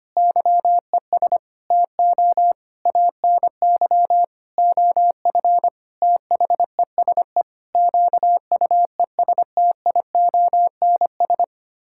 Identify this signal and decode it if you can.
Morse code, heard as a keyed tone, about 25 wpm, 700 Hz. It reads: YES TO ANY OF THESE QUESTIONS
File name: Sets of 9 Words - 25wpm